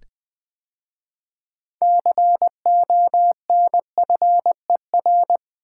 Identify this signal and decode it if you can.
Morse code, heard as a keyed tone, about 20 wpm, 700 Hz. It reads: CONFER
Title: 6 Letter Words - First 3 Letters Given - 20wpm